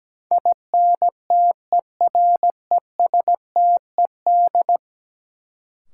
Morse code, heard as a keyed tone, about 17 wpm, 700 Hz. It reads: INTERESTED
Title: Sentences from Top 900 Words - New Word Drill - 17wpm